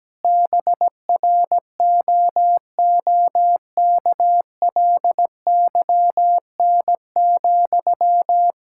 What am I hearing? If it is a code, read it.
Morse code, heard as a keyed tone, about 17 wpm, 700 Hz. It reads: BROOKLYN,
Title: Practice QSO - Exchange 1 - Beginning - 17wpm